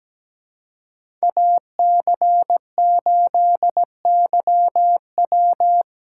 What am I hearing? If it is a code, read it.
Morse code, heard as a keyed tone, about 20 wpm, 700 Hz. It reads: AC8YW
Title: Speed Racing - USA Call Signs 17wpm